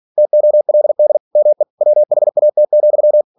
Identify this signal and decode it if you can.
Morse code, heard as a keyed tone, about 46 wpm, 600 Hz. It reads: TOLD ME WHAT,
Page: Never 47wpm